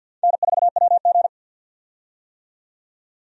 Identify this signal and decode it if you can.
Morse code, heard as a keyed tone, about 50 wpm, 700 Hz. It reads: N4WG
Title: USA Call Signs - 4 Characters - 50wpm